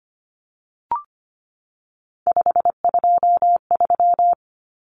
Morse code, heard as a keyed tone, about 25 wpm, 700 Hz. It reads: 523
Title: Numbers - 3-Digit 25wpm